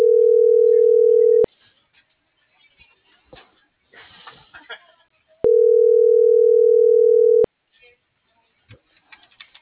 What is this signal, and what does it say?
Ringback tone (North America), 440 + 480 Hz.